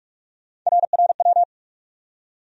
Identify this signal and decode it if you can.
Morse code, heard as a keyed tone, about 46 wpm, 700 Hz. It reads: RRW
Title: USA Call Sign Postfixes - 3 Letters 45wpm